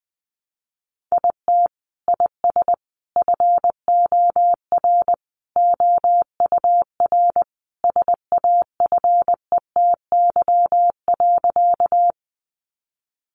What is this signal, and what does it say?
Morse code, heard as a keyed tone, about 20 wpm, 700 Hz. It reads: IT IS FOR OUR SAFETY.
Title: Sentences from Top 700 Words 20wpm